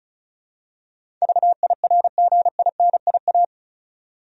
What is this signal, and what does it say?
Morse code, heard as a keyed tone, about 34 wpm, 700 Hz. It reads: VIRGINIA
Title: USA Common Names - 35wpm